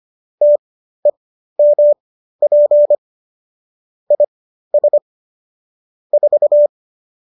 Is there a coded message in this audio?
Yes: Morse code from 0.4 to 6.7 s, heard as a keyed tone, about 25 wpm, spaced as for 13 wpm, 590 Hz. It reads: TEMP IS 4